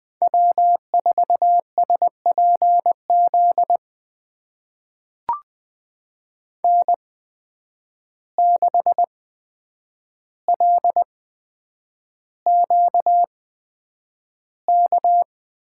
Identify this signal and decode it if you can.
Morse code, heard as a keyed tone, about 20 wpm, spaced as for 9 wpm, 700 Hz. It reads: W4SPZ N6LQK
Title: USA Call Signs - 5 Characters - 8x Character Spacing - 20wpm